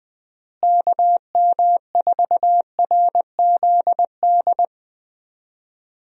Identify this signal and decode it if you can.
Morse code, heard as a keyed tone, about 20 wpm, 700 Hz. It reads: KM4RZD
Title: USA Call Signs - 6 Characters - Repeated 2x - 20wpm